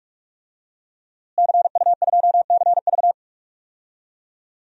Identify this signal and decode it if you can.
Morse code, heard as a keyed tone, about 42 wpm, 700 Hz. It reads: CU2CV
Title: DXCC-100 Call Signs - 5 Characters - Repeated 3x - 45wpm